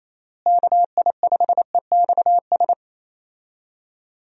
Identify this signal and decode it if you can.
Morse code, heard as a keyed tone, about 28 wpm, 700 Hz. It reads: KI5EXS
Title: USA Call Signs - 6 Characters - Repeated 3x - 28wpm